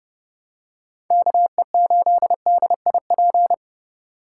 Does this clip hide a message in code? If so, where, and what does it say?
Yes: Morse code from 1.1 to 3.5 s, heard as a keyed tone, about 29 wpm, 700 Hz. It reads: KE8DIP